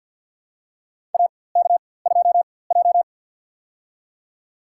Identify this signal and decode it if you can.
Morse code, heard as a keyed tone, about 47 wpm, spaced as for 23 wpm, 700 Hz. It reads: AK2J
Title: USA Call Signs - 4x Character Spacing - 50wpm